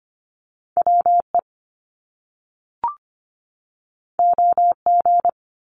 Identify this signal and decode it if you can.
Morse code, heard as a keyed tone, about 25 wpm, 700 Hz. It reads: WE OG
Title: Top Two Letter Combinations in Words 25wpm